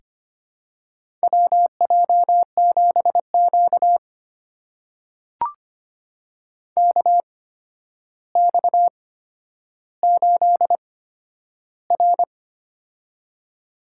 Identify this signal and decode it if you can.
Morse code, heard as a keyed tone, about 25 wpm, spaced as for 11 wpm, 700 Hz. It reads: WJ7Q KX8R
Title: USA Call Signs - 4 Characters - 8x Character Spacing - 25wpm